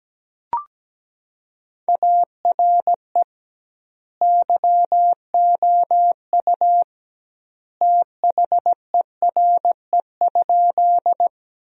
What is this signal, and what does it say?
Morse code, heard as a keyed tone, about 17 wpm, 700 Hz. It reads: ARE YOU THERE?
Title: 2x Word-Spacing - Sentences from Top 100 Words - 17wpm